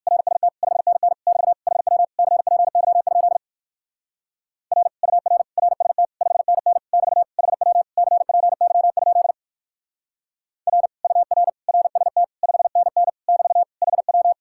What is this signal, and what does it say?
Morse code, heard as a keyed tone, about 52 wpm, 700 Hz. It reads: RST 5NN <BT> HW CPY? R UR RST 5NN <BT> HW CPY? R UR RST 5NN <BT> HW
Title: Speed-Racing Common QSO Phrases - 45wpm